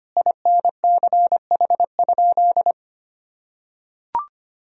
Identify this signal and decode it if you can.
Morse code, heard as a keyed tone, about 25 wpm, 700 Hz. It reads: INCH?
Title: Sentences from Top 1000 Words - Review Part 3 - 25wpm